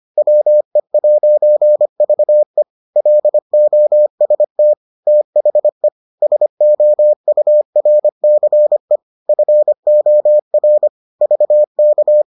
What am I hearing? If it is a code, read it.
Morse code, heard as a keyed tone, about 25 wpm, 600 Hz. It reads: WE'VE LOST THE SOURCE FOR VK